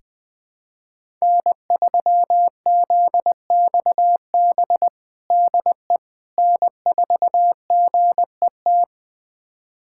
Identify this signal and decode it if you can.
Morse code, heard as a keyed tone, about 20 wpm, 700 Hz. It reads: N3ZXB DE N4GET
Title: USA Callsign DE Callsign - 5 Characters - 20wpm